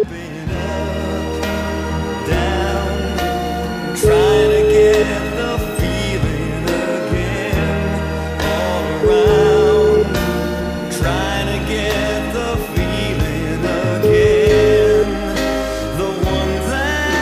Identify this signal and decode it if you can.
Ringback tone (Europe, 425 Hz), with noise.